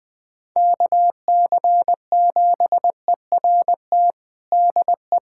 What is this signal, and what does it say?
Morse code, heard as a keyed tone, about 21 wpm, 700 Hz. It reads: KC7ERT DE
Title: USA Callsign DE Callsign - 6 Characters - 20wpm